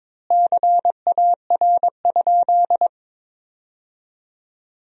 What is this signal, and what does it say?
Morse code, heard as a keyed tone, about 22 wpm, 700 Hz. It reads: CAR?
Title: 2x Word-Spacing - Sentences from Top 300 Words - 22wpm